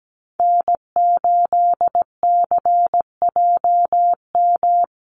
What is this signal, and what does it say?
Morse code, heard as a keyed tone, about 17 wpm, 700 Hz. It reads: N8CJM
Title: USA Call Signs Repeated 3x 17wpm